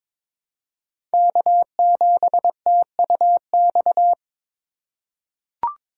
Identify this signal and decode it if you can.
Morse code, heard as a keyed tone, about 22 wpm, 700 Hz. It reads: K7TUX
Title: USA Call Signs - 5 Characters - 4x Character Spacing - 22wpm